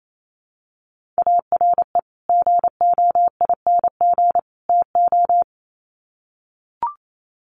Morse code, heard as a keyed tone, about 28 wpm, 700 Hz. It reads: ARE GOING TO
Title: Sets of 3 Words 28wpm